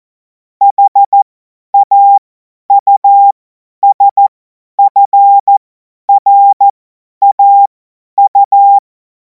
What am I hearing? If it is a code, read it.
Morse code, heard as a keyed tone, about 14 wpm, spaced as for 10 wpm, 800 Hz. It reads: HAUSFRAU
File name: Morsekurs / Witze / Witz 10. März 2026 50 F70